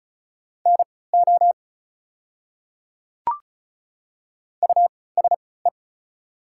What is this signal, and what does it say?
Morse code, heard as a keyed tone, about 34 wpm, spaced as for 20 wpm, 690 Hz. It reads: NO USE
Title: 3x Character Spacing - Top 100 Words - 35wpm